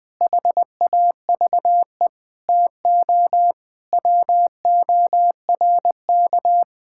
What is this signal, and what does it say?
Morse code, heard as a keyed tone, about 20 wpm, 700 Hz. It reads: HAVE TO WORK
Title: Sentences with 5-Word Phrases - Spoken Beginning - 20wpm